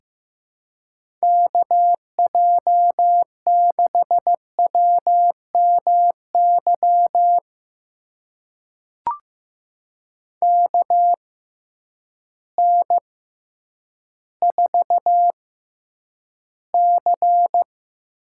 Morse code, heard as a keyed tone, about 15 wpm, spaced as for 9 wpm, 700 Hz. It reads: KJ6WMY KN4C
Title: USA Call Signs - 6 Characters - 6x Character Spacing - 15wpm